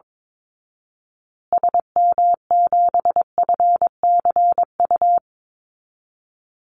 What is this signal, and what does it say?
Morse code, heard as a keyed tone, about 22 wpm, 700 Hz. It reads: SM7FCU
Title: Call Signs Repeated 2x - 22wpm